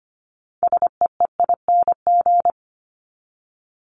Morse code, heard as a keyed tone, about 25 wpm, 700 Hz. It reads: SEEING
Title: ING Suffix - Top 100 Words with Base Words 25wpm